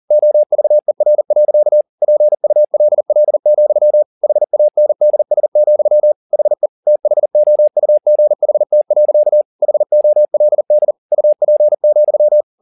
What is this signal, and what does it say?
Morse code, heard as a keyed tone, about 40 wpm, 600 Hz. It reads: OVER. PULL, HANDS, HE THOUGHT. HOLD UP,
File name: the old man and the sea 2 40wpm